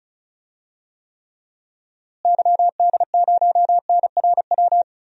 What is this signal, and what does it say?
Morse code, heard as a keyed tone, about 35 wpm, 700 Hz. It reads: YD0NRW
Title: DXCC-100 Call Signs - 6 Characters - Repeated 3x - 35wpm